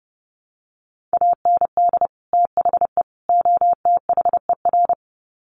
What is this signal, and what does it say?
Morse code, heard as a keyed tone, about 30 wpm, 700 Hz. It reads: AND THE OTHER
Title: Sets of 3 Words 30wpm